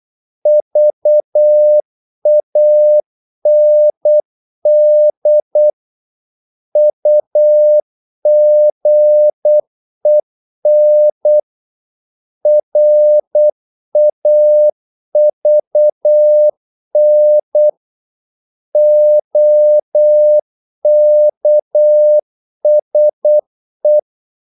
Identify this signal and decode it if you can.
Morse code, heard as a keyed tone, about 8 wpm, 600 Hz. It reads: VAND UGEN RAVN OKSE
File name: Korte ord DK 8wpm